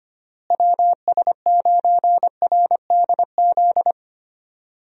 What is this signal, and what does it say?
Morse code, heard as a keyed tone, about 25 wpm, 700 Hz. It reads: WS9RDZ
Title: USA Call Signs - 6 Characters - Repeated 2x - 25wpm